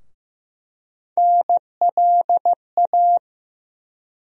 Morse code, heard as a keyed tone, about 15 wpm, 700 Hz. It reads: NLA
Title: USA Call Sign Postfixes - 3 Letters 15wpm